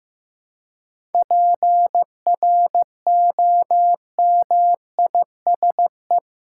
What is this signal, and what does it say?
Morse code, heard as a keyed tone, about 15 wpm, 700 Hz. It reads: PROMISE
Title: Seven Letter Words - 15wpm